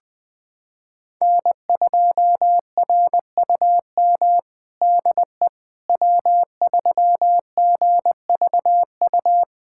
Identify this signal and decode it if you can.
Morse code, heard as a keyed tone, about 20 wpm, 700 Hz. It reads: N2RUM DE W3GVU
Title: USA Callsign DE Callsign - 5 Characters - 20wpm